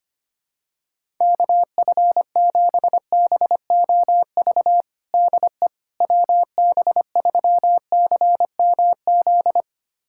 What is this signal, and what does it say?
Morse code, heard as a keyed tone, about 25 wpm, 700 Hz. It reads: KF7BOV DE WB3CMZ
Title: USA Callsign DE Callsign 25wpm